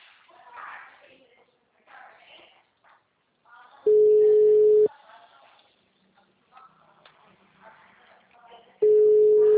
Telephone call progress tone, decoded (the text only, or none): ringback tone